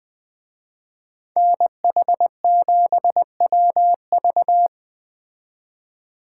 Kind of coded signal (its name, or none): Morse code